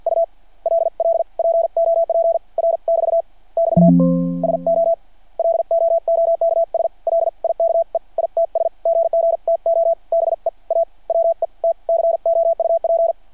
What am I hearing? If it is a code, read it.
Morse code, heard as a keyed tone, about 51 wpm, spaced as for 41 wpm, 690 Hz. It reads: ULL POP R <BT> <BT> SO LOOKS LIKE ITS GG TO BE A WET COUP
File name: archived / CRSnet